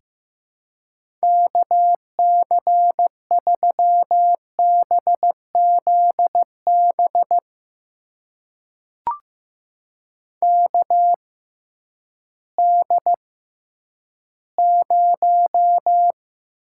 Morse code, heard as a keyed tone, about 15 wpm, spaced as for 10 wpm, 700 Hz. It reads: KC3BZB KD0
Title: USA Call Signs - 6 Characters - 6x Character Spacing - 15wpm